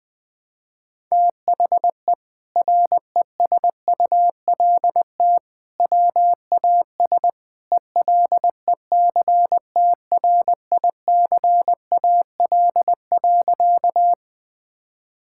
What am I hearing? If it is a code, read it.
Morse code, heard as a keyed tone, about 20 wpm, 700 Hz. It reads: THE RESULT WAS ELECTRICAL.
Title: Sentences from Top 1000 Words - Review Part 6 - 20wpm